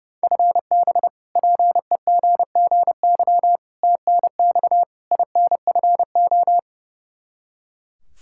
Morse code, heard as a keyed tone, about 30 wpm, 700 Hz. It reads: FB PEGGY TNX INFO
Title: Practice QSO - Exchange 3 - More Info - 30wpm